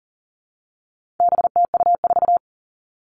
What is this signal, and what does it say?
Morse code, heard as a keyed tone, about 40 wpm, 700 Hz. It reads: BTU4
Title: All Cut Numbers - 4-Digit 40wpm